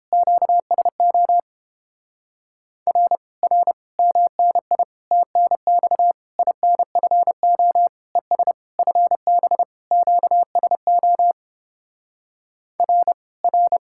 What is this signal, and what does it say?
Morse code, heard as a keyed tone, about 31 wpm, 700 Hz. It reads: QSO R R MNI TNX INFO ES FB QSO R R